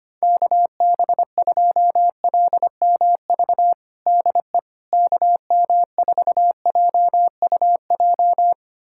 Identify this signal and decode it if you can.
Morse code, heard as a keyed tone, about 25 wpm, 700 Hz. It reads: KB2LMV DE KM4JUJ
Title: USA Callsign DE Callsign - 6 Characters - 25wpm